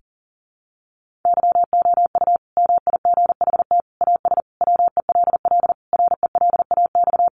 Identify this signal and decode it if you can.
Morse code, heard as a keyed tone, about 40 wpm, 700 Hz. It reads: YOU MIGHT AS WELL RELAX
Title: Sentences from Top 1400 Words 40wpm